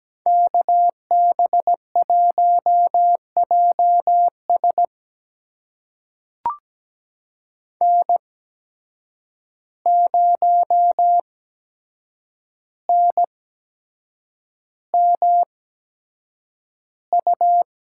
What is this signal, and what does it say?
Morse code, heard as a keyed tone, about 17 wpm, spaced as for 7 wpm, 700 Hz. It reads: KB1JS N0NMU
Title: USA Call Signs - 5 Characters - 8x Character Spacing - 17wpm